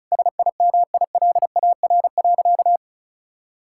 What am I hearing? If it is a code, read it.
Morse code, heard as a keyed tone, about 36 wpm, 690 Hz. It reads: SIMILAR.
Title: Sentences from Top 1000 Words - Review Part 5 - 35wpm